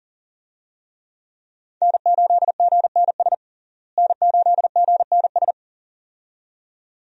Morse code, heard as a keyed tone, about 37 wpm, 700 Hz. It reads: N8GNS N8GNS
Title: USA Call Signs - 5 Characters - Repeated 2x - 40wpm